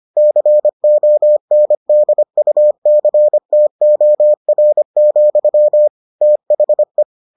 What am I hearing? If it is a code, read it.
Morse code, heard as a keyed tone, about 25 wpm, 600 Hz. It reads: CONDUCTOR, THE